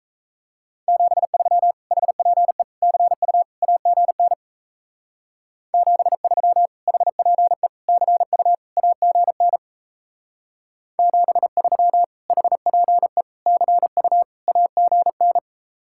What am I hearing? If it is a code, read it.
Morse code, heard as a keyed tone, about 33 wpm, 700 Hz. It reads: 73 HPE CU AGN 73 HPE CU AGN 73 HPE CU AGN
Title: Speed-Racing Common QSO Phrases Encore - 28wpm